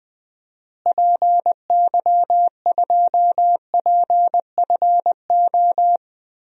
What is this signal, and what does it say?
Morse code, heard as a keyed tone, about 20 wpm, 700 Hz. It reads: PY2PFO
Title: DXCC-100 Call Signs - 6 Characters - Repeated 2x - 20wpm